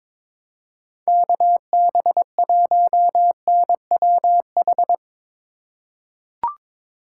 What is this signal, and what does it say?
Morse code, heard as a keyed tone, about 21 wpm, 700 Hz. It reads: KB1NWH
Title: USA Call Signs - 6 Characters - 3x Character Spacing - 22wpm